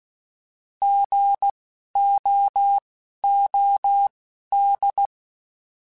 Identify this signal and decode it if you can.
Morse code, heard as a keyed tone, about 16 wpm, spaced as for 11 wpm, 800 Hz. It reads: GOOD
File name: Alphabetical / Alphabetical mp3s / Gs @10